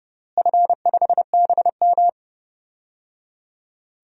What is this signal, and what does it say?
Morse code, heard as a keyed tone, about 30 wpm, 700 Hz. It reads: F5BM
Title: DXCC-100 Call Signs - 4 Characters - Repeated 3x - 30wpm